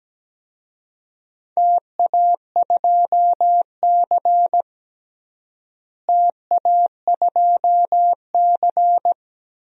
Morse code, heard as a keyed tone, about 17 wpm, 700 Hz. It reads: TA2C TA2C
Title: DXCC-100 Call Signs - 4 Characters - Repeated 3x - 17wpm